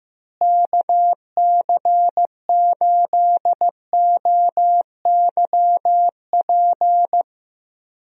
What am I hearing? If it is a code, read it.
Morse code, heard as a keyed tone, about 15 wpm, 700 Hz. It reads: KC8OYP
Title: USA Call Signs - 6 Characters - Repeated 2x - 15wpm